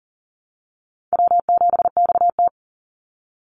Morse code, heard as a keyed tone, about 39 wpm, 700 Hz. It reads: W7XT
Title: Easy Call Signs Repeated 3x - 40wpm